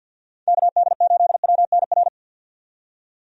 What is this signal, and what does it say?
Morse code, heard as a keyed tone, about 49 wpm, 700 Hz. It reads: KD8WNR
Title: USA Call Signs - 6 Characters - 50wpm